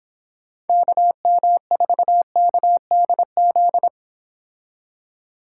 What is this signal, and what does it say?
Morse code, heard as a keyed tone, about 29 wpm, 700 Hz. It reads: KM4KDZ